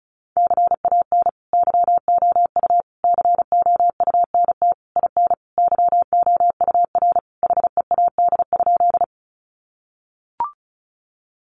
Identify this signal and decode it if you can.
Morse code, heard as a keyed tone, about 34 wpm, 700 Hz. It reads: CAN YOU COUNT IN YOUR HEAD?
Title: Sentences from Top 1500 Words 35wpm